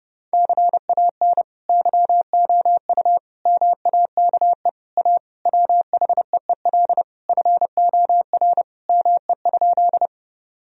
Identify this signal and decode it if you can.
Morse code, heard as a keyed tone, about 31 wpm, 700 Hz. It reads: CAN YOU MAKE A WHEEL FOR ME?
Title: Sentences from Top 1000 Words - Review Part 3 - 30wpm